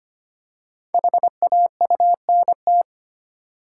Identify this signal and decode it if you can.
Morse code, heard as a keyed tone, about 25 wpm, 700 Hz. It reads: HAUNT
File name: 5 Letter Words - First 3 Letters Given - 25wpm